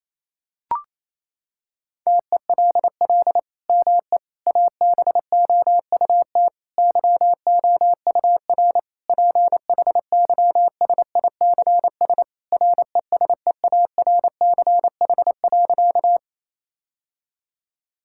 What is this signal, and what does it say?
Morse code, heard as a keyed tone, about 28 wpm, 700 Hz. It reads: TELL ME ABOUT YOUR PHYSICS RESEARCH.
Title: Sentences from Top 1000 Words - Review Part 5 - 28wpm